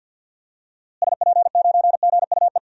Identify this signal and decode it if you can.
Morse code, heard as a keyed tone, about 46 wpm, 700 Hz. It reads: SP9GUE
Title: DXCC-100 Call Signs - 6 Characters - Repeated 3x - 50wpm